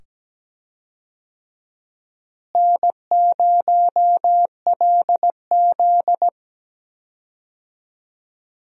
Morse code, heard as a keyed tone, about 17 wpm, 700 Hz. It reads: N0LZ
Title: USA Call Signs - 4 Characters - Repeated 2x - 17wpm